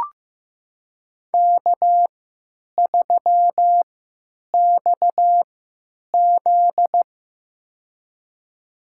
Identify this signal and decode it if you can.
Morse code, heard as a keyed tone, about 15 wpm, spaced as for 9 wpm, 700 Hz. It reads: K3XZ
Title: USA Call Signs - 4 Characters - 3x Character Spacing - 15wpm